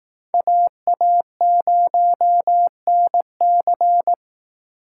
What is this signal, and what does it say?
Morse code, heard as a keyed tone, about 18 wpm, 700 Hz. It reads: AA0NC
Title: Speed Racing - USA Call Signs 15wpm